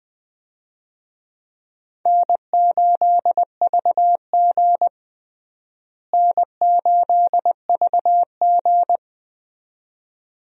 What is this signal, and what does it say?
Morse code, heard as a keyed tone, about 20 wpm, 700 Hz. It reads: N8VG N8VG
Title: USA Call Signs - 4 Characters - Repeated 2x - 20wpm